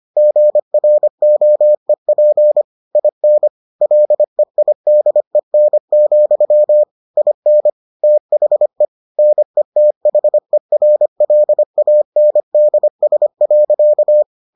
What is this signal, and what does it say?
Morse code, heard as a keyed tone, about 25 wpm, 600 Hz. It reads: GROEP IN LEIDEN, IN THE NETHERLANDS.